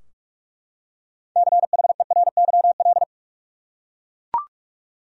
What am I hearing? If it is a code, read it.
Morse code, heard as a keyed tone, about 42 wpm, 700 Hz. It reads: CHERYL
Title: USA Common Names - 45wpm